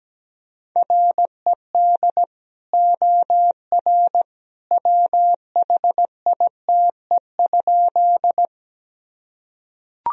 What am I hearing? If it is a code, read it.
Morse code, heard as a keyed tone, about 17 wpm, 700 Hz. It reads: RED OR WHITE?
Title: Sentences from Top 1000 Words - Review Part 2 - 17wpm